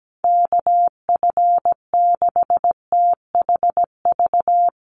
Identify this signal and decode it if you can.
Morse code, heard as a keyed tone, about 17 wpm, 700 Hz. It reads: KF6THV
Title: USA Call Signs Repeated 2x 17wpm